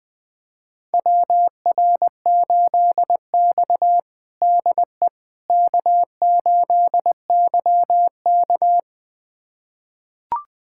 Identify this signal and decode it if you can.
Morse code, heard as a keyed tone, about 20 wpm, 700 Hz. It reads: WR8X DE K8YK
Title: USA Callsign DE Callsign - 4 Characters - 20wpm